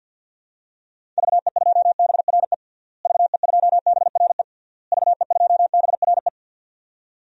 Morse code, heard as a keyed tone, about 48 wpm, 700 Hz. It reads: VE2BRE VE2BRE VE2BRE
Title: DXCC-100 Call Signs - 6 Characters - Repeated 3x - 50wpm